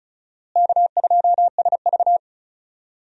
Morse code, heard as a keyed tone, about 34 wpm, 700 Hz. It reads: K2SV
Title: USA Call Signs - 4 Characters - Repeated 3x - 35wpm